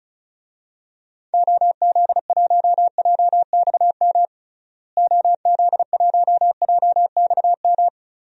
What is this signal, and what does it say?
Morse code, heard as a keyed tone, about 34 wpm, 700 Hz. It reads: OZ1JXM OZ1JXM
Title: DXCC-100 Call Signs - 6 Characters - Repeated 2x - 35wpm